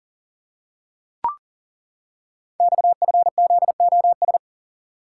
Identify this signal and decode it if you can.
Morse code, heard as a keyed tone, about 39 wpm, 700 Hz. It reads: XFZOS
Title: Code Groups - 5 Letters - 40wpm